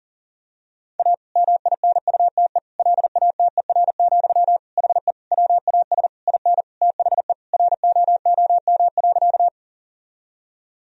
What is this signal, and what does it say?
Morse code, heard as a keyed tone, about 40 wpm, 700 Hz. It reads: A MINUTE LATER, HE WAS IN THE ROOM.
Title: Sentences from Top 1000 Words - Review Part 4 - 40wpm